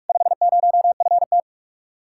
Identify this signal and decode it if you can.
Morse code, heard as a keyed tone, about 59 wpm, 700 Hz. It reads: 50FT